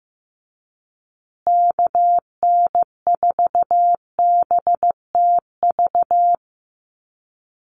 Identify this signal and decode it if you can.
Morse code, heard as a keyed tone, about 15 wpm, 700 Hz. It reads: KN4BTV
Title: USA Call Signs Repeated 2x 15wpm